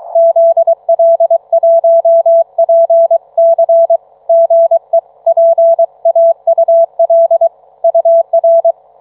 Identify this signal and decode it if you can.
Morse code, heard as a keyed tone, about 23 wpm, 660 Hz. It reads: ZL1PC GE PAUL UR